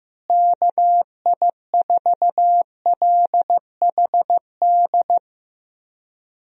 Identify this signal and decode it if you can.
Morse code, heard as a keyed tone, about 15 wpm, 700 Hz. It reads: KI4LHD
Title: USA Call Signs - 6 Characters - Repeated 2x - 15wpm